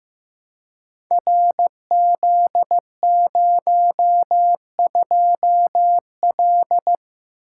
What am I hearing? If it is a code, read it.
Morse code, heard as a keyed tone, about 15 wpm, 700 Hz. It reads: RZ02L
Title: Code Groups - 5 Letter:Numbers - 15wpm